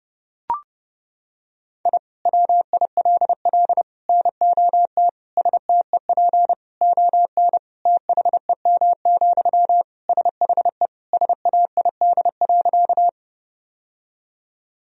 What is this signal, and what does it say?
Morse code, heard as a keyed tone, about 29 wpm, 700 Hz. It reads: I WILL NOT STEP ON THEM, SHE SAID.
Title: Sentences from Top 1000 Words - Review Part 2 - 30wpm